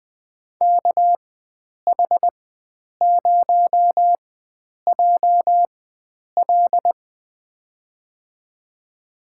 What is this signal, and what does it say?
Morse code, heard as a keyed tone, about 20 wpm, spaced as for 9 wpm, 700 Hz. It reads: KH0JL
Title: USA Call Signs - 5 Characters - 4x Character Spacing - 20wpm